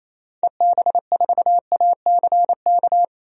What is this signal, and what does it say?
Morse code, heard as a keyed tone, about 28 wpm, 700 Hz. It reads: EB4ACK
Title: DXCC-100 Call Signs - 6 Characters - 28wpm